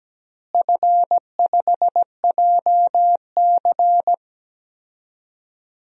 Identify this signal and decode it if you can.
Morse code, heard as a keyed tone, about 17 wpm, 700 Hz. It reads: F5JC
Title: DXCC-100 Call Signs - 4 Characters - Repeated 3x - 17wpm